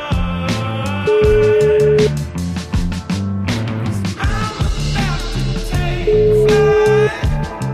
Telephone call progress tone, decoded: ringback tone

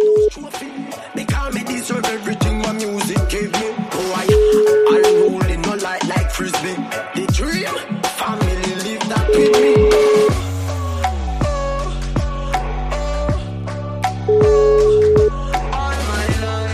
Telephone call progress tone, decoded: ringback tone

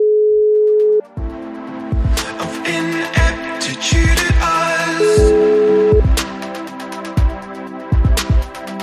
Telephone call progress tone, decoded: ringback tone

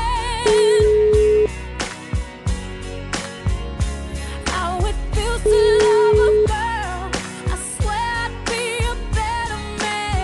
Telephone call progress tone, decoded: ringback tone

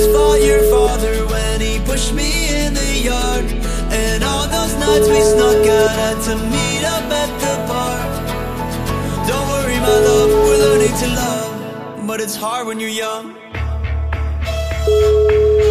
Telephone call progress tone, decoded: ringback tone

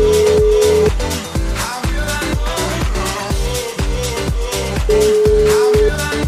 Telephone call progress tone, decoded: ringback tone

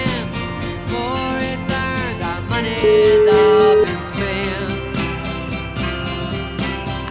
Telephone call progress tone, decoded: ringback tone

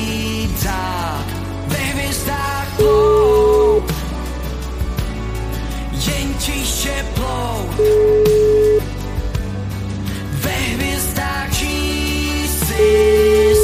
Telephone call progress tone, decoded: ringback tone

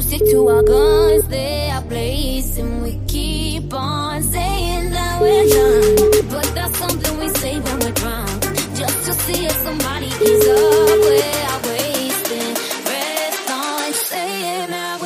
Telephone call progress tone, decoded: ringback tone

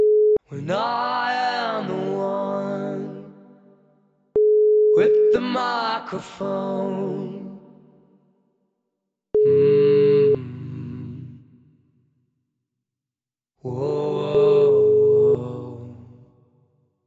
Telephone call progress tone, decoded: ringback tone

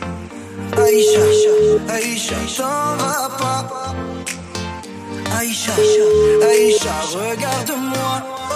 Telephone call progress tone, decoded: ringback tone

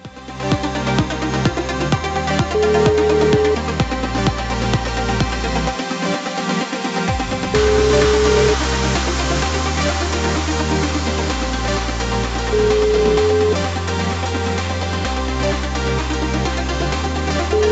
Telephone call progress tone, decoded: ringback tone

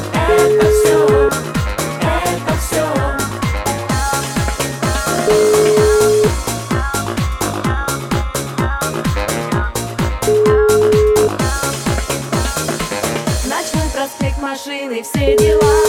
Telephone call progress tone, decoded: ringback tone

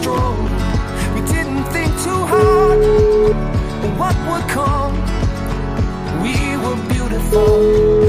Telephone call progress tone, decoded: ringback tone

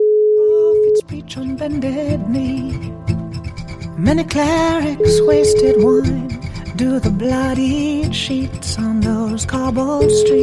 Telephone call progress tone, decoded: ringback tone

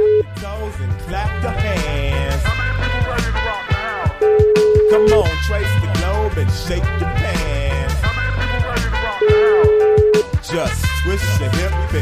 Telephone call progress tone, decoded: ringback tone

